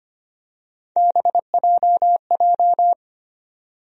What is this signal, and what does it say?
Morse code, heard as a keyed tone, about 25 wpm, 700 Hz. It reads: BJJ
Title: USA Call Sign Postfixes - 3 Letters 25wpm